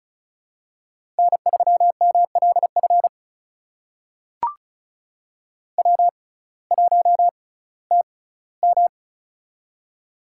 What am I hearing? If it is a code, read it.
Morse code, heard as a keyed tone, about 36 wpm, spaced as for 19 wpm, 700 Hz. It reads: N3MLF W1TM
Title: USA Call Signs - 6x Character Spacing - 35wpm